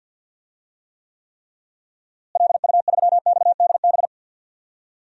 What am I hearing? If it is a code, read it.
Morse code, heard as a keyed tone, about 48 wpm, 700 Hz. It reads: LU3XDB
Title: DXCC-100 Call Signs - 6 Characters - Repeated 3x - 50wpm